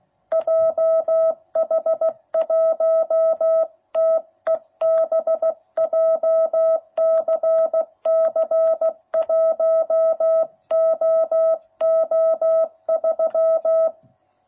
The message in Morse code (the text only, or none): JH1TEB JCC 1OO3